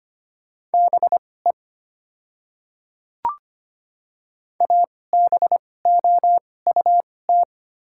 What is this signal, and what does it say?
Morse code, heard as a keyed tone, about 25 wpm, spaced as for 18 wpm, 700 Hz. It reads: BE ABOUT
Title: 2x Character Spacing - Top 300 Words - 25wpm